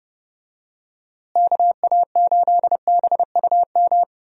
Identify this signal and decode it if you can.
Morse code, heard as a keyed tone, about 31 wpm, 700 Hz. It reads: KA8BUM
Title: USA Call Signs - 6 Characters - Repeated 3x - 30wpm